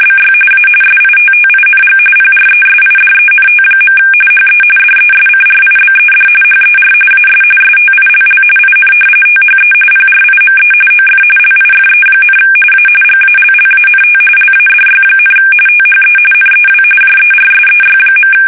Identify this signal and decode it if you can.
RTTY, about 45.45 baud. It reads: BY A SITTING PRESIDENT. HE DESCRIBED HER AS STILL IN THE PRIME OF LIFE, A GALLANT LADY, THE NEW JERSEY.
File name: fsk850 / battleshipnj